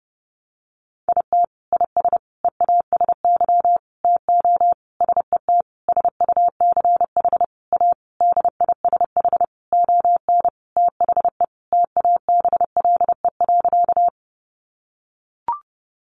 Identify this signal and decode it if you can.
Morse code, heard as a keyed tone, about 30 wpm, 700 Hz. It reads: IT IS EASY TO SET SUCH A DISH ON THE TABLE.
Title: Sentences from Top 1300 Words 30wpm